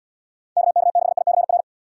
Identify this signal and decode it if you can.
Morse code, heard as a keyed tone, about 77 wpm, 700 Hz. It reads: KK6ICR